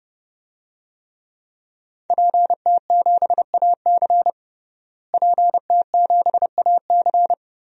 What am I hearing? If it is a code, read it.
Morse code, heard as a keyed tone, about 30 wpm, 700 Hz. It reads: PT7AC PT7AC
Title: DXCC-100 Call Signs - 5 Characters - Repeated 3x - 30wpm